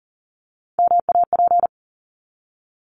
Morse code, heard as a keyed tone, about 40 wpm, 700 Hz. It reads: MAP
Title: Top 500 Words 40wpm